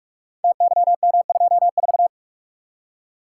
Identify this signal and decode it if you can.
Morse code, heard as a keyed tone, about 44 wpm, 700 Hz. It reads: TYM24
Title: Code Groups - 5 Letter:Numbers - 45wpm